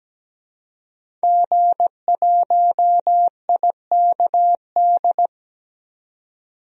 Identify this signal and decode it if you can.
Morse code, heard as a keyed tone, about 17 wpm, 700 Hz. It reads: G1IKD
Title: DXCC-100 Call Signs - 5 Characters - 17wpm